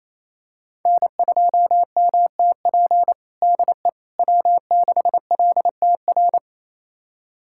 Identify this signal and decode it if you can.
Morse code, heard as a keyed tone, about 28 wpm, 700 Hz. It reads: N2MTP DE W6LTR
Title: USA Callsign DE Callsign - 5 Characters - 28wpm